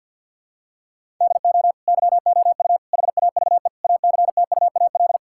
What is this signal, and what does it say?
Morse code, heard as a keyed tone, about 50 wpm, 700 Hz. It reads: DO YOU HAVE ACTUAL
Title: Sentences from Top 1000 Words - Review Part 5 - 50wpm